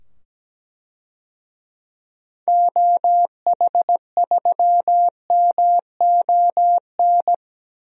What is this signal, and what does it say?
Morse code, heard as a keyed tone, about 17 wpm, 700 Hz. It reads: OH3MON
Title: DXCC-100 Call Signs - 6 Characters - 17wpm